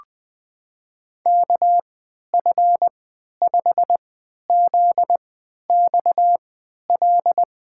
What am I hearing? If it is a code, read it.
Morse code, heard as a keyed tone, about 20 wpm, spaced as for 11 wpm, 700 Hz. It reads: KF5ZXL
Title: USA Call Signs - 6 Characters - 3x Character Spacing - 20wpm